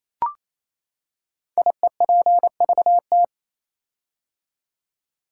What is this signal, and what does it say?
Morse code, heard as a keyed tone, about 27 wpm, 700 Hz. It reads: IEPVT
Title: Code Groups - 5 Letters - 28wpm